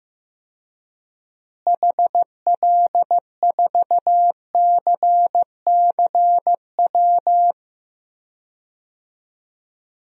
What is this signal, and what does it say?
Morse code, heard as a keyed tone, about 15 wpm, 700 Hz. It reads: HL4CCW
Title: DXCC-100 Call Signs - 6 Characters - Repeated 3x - 15wpm